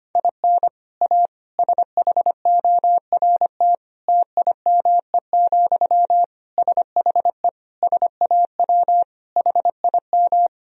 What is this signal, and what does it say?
Morse code, heard as a keyed tone, about 25 wpm, 700 Hz. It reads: IN A SHORT TIME, SHE SAW HIM
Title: Sentences from Top 1000 Words - Review Part 2 - 25wpm